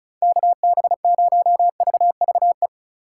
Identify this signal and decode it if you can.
Morse code, heard as a keyed tone, about 36 wpm, 700 Hz. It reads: KB0VVE
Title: USA Call Signs - 8x Character Spacing - 35wpm